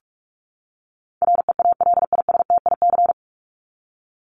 Morse code, heard as a keyed tone, about 44 wpm, 700 Hz. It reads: REALISTIC
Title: Top 1751-2000 Words 45wpm